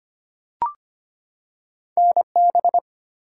Morse code, heard as a keyed tone, about 25 wpm, 700 Hz. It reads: NB
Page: morse-code-ninja-practice-25wpm